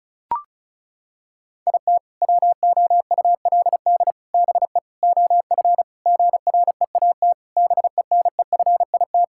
Morse code, heard as a keyed tone, about 34 wpm, 690 Hz. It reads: IT WOULD BE OF GREAT BENEFIT
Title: Sentences from Top 1000 Words - Review Part 6 - 35wpm